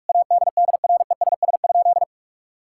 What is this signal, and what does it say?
Morse code, heard as a keyed tone, about 46 wpm, 700 Hz. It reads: ADDRESS?
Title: Sentences from Top 1000 Words - Review Part 6 - 45wpm